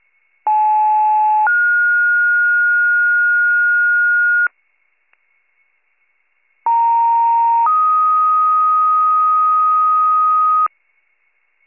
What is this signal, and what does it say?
Two-tone page, heard as a one-second tone, then a three-second tone, about 860 then 1430 Hz, about 930 then 1250 Hz. It is heard from 0.5 to 10.7 s.